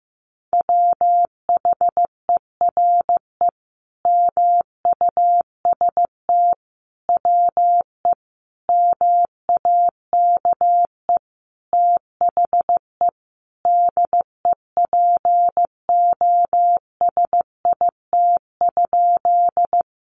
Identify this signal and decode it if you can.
Morse code, heard as a keyed tone, about 15 wpm, 700 Hz. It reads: WHERE MUST WE MAKE THE DEPOSIT?
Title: Sentences from Top 1300 Words 15wpm